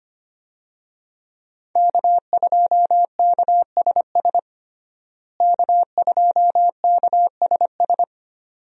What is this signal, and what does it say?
Morse code, heard as a keyed tone, about 25 wpm, 700 Hz. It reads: K2KSS K2KSS
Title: USA Call Signs - 5 Characters - Repeated 2x - 25wpm